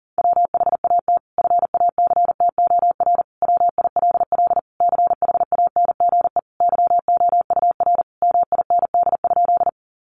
Morse code, heard as a keyed tone, about 37 wpm, 700 Hz. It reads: WHAT FACTOR WILL CHANGE YOUR MIND?
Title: Sentences from Top 1300 Words 40wpm